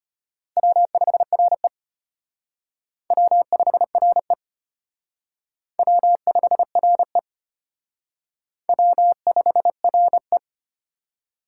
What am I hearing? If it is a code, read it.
Morse code, heard as a keyed tone, about 30 wpm, 700 Hz. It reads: W5RE W5RE W5RE W5RE
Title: Speed Racing - USA 1x2 Call Signs 25wpm